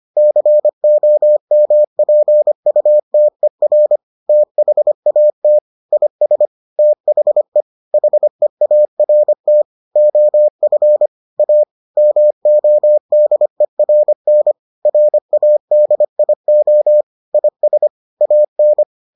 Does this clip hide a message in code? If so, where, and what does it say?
Yes: Morse code from 0.2 to 18.8 s, heard as a keyed tone, about 25 wpm, 600 Hz. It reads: COMPUTER THAT IS THE HEART OF A MODERN RADIO IS AN